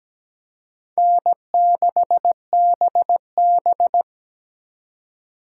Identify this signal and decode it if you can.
Morse code, heard as a keyed tone, about 20 wpm, 700 Hz. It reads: N6BB
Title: Speed Racing - USA 1x2 Call Signs 17wpm